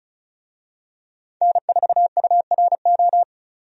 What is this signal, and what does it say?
Morse code, heard as a keyed tone, about 34 wpm, 700 Hz. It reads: N4URO
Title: USA Call Signs - 5 Characters - 35wpm